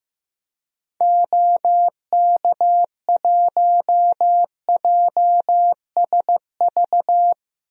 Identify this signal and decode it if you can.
Morse code, heard as a keyed tone, about 16 wpm, 700 Hz. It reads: OK1JSV
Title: DXCC-100 Call Signs - 6 Characters - Repeated 2x - 15wpm